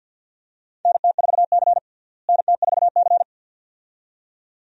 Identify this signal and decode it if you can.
Morse code, heard as a keyed tone, about 50 wpm, 700 Hz. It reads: NT4C NT4C
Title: USA Call Signs - 4 Characters - Repeated 2x - 50wpm